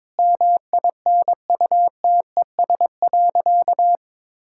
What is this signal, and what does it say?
Morse code, heard as a keyed tone, about 22 wpm, 700 Hz. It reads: MINUTES.
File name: practice / 22wpm / Sentences with 3-Word Phrases - 22wpm